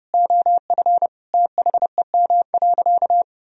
Morse code, heard as a keyed tone, about 30 wpm, 700 Hz. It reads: OF THEM.
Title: Sentences from Top 1000 Words - Review Part 2 - 30wpm